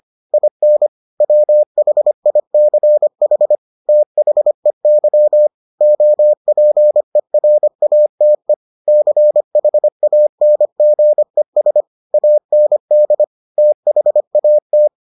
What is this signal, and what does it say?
Morse code, heard as a keyed tone, about 25 wpm, 600 Hz. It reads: IN WHICH THEY OPERATE CHANGES AND THAT